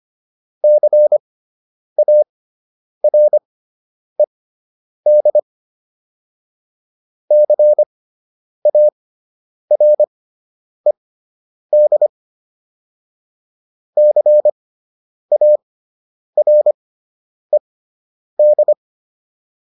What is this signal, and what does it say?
Morse code, heard as a keyed tone, about 25 wpm, spaced as for 9 wpm, 600 Hz. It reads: CARED CARED CARED